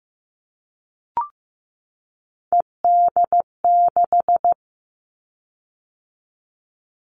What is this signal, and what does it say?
Morse code, heard as a keyed tone, about 15 wpm, 700 Hz. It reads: ED6
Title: Callsign Prefixes for Top 100 DXCC - 15wpm